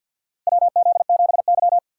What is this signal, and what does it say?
Morse code, heard as a keyed tone, about 53 wpm, 700 Hz. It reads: WZ7Y
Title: USA Call Signs - 3x Character Spacing - 50wpm